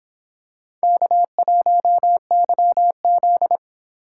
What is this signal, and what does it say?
Morse code, heard as a keyed tone, about 26 wpm, 700 Hz. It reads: K1YZ